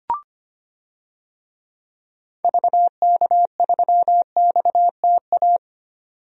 Morse code, heard as a keyed tone, about 25 wpm, 690 Hz. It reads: VK3XTA
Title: DXCC-100 Call Signs - 6 Characters - 25wpm